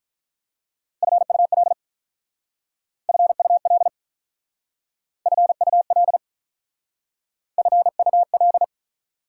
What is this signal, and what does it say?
Morse code, heard as a keyed tone, about 47 wpm, 690 Hz. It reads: FUL FUL FUL FUL